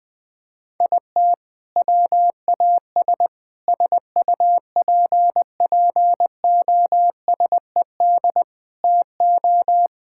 Morse code, heard as a keyed tone, about 20 wpm, 700 Hz. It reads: IT WAS SUPPOSED TO
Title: Sentences with 4-Word Phrases - 20wpm